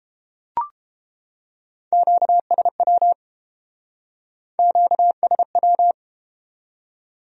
Morse code, heard as a keyed tone, about 31 wpm, 700 Hz. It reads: QSW QSW